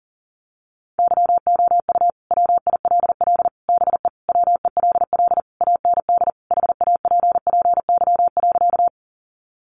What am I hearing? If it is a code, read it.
Morse code, heard as a keyed tone, about 37 wpm, 700 Hz. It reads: YOU WILL BE WELL AND HAPPY.
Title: Sentences from Top 1000 Words 40wpm